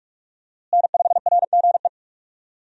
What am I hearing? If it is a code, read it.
Morse code, heard as a keyed tone, about 46 wpm, 700 Hz. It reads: N5RGE
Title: USA Call Signs - 5 Characters - Repeated 2x - 45wpm